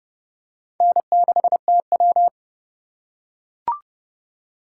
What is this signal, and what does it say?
Morse code, heard as a keyed tone, about 30 wpm, 700 Hz. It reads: N6TW
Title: K1USN SST QSO Practice - 30wpm